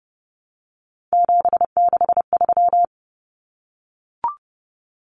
Morse code, heard as a keyed tone, about 30 wpm, 700 Hz. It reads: 763
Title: Numbers - 3-Digit 30wpm